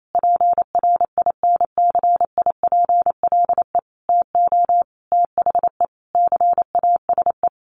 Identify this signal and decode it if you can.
Morse code, heard as a keyed tone, about 28 wpm, 700 Hz. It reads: PRINCIPLE TO THE CASE
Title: Sentences from Top 1400 Words 28wpm